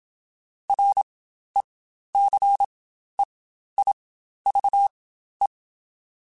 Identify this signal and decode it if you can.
Morse code, heard as a keyed tone, about 27 wpm, spaced as for 12 wpm, 800 Hz. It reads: RECEIVE
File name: Quotes Spaced / Quotes Spaced-mp3s / Advice sp @25